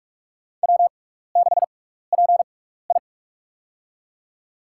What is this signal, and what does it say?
Morse code, heard as a keyed tone, about 44 wpm, spaced as for 15 wpm, 700 Hz. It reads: W6PI
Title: USA Call Signs - 4 Characters - 6x Character Spacing - 45wpm